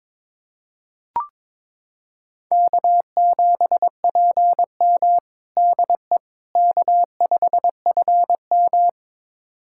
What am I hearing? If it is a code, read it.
Morse code, heard as a keyed tone, about 22 wpm, 700 Hz. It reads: K7PM DE K5FM
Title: USA Callsign DE Callsign - 4 Characters - 22wpm